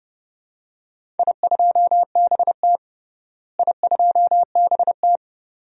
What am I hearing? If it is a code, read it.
Morse code, heard as a keyed tone, about 32 wpm, 700 Hz. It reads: I2BT I2BT
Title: DXCC-100 Call Signs - 4 Characters - Repeated 2x - 30wpm